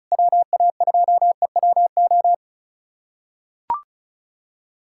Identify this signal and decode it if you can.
Morse code, heard as a keyed tone, about 34 wpm, 700 Hz. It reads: WA2EWO
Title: USA Callsign DE Callsign 35wpm